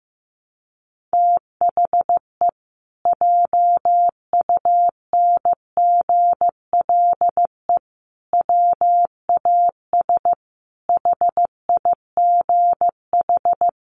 Morse code, heard as a keyed tone, about 15 wpm, 700 Hz. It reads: THE JUNGLE WAS HIGH
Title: A Sound Of Thunder 15wpm